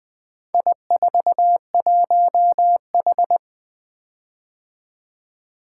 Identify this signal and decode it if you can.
Morse code, heard as a keyed tone, about 23 wpm, 700 Hz. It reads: I41H
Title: Speed-Racing - Four Letter-Number Combinations - 20wpm